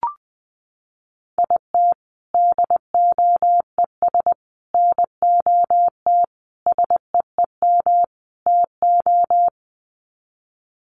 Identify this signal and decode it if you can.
Morse code, heard as a keyed tone, about 19 wpm, 700 Hz. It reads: IT DOES NOT SEEM TO
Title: Sets of 5 Words Encore2 20wpm